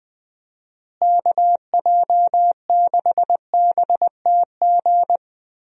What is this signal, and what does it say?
Morse code, heard as a keyed tone, about 20 wpm, 700 Hz. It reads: KJ6BTG
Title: USA Call Signs - 6 Characters - Repeated 3x - 20wpm